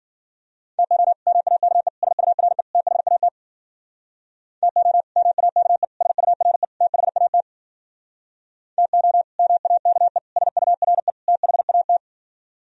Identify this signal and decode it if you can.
Morse code, heard as a keyed tone, about 51 wpm, 700 Hz. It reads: TO MAKE SURE THAT TO MAKE SURE THAT TO MAKE SURE THAT